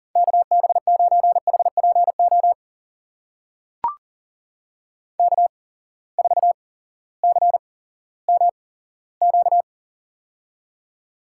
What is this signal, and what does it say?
Morse code, heard as a keyed tone, about 37 wpm, spaced as for 18 wpm, 700 Hz. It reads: KB9HPO K4CMQ
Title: USA Call Signs - 8x Character Spacing - 40wpm